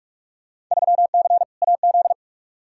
Morse code, heard as a keyed tone, about 42 wpm, 700 Hz. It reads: 3C AZ
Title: Field Day Exchanges Encore - 45wpm